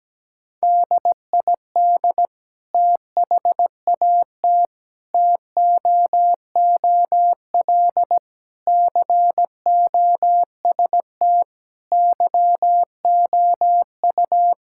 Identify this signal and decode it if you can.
Morse code, heard as a keyed tone, about 17 wpm, 700 Hz. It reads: DID THAT TOOL COST YOU
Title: Sentences from Top 1000 Words - Review Part 6 - 17wpm